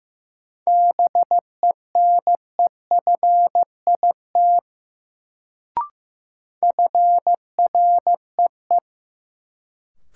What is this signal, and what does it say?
Morse code, heard as a keyed tone, about 15 wpm, 700 Hz. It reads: BENEFIT FREE
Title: Sets of 5 Words Encore - New Word Drill - 15wpm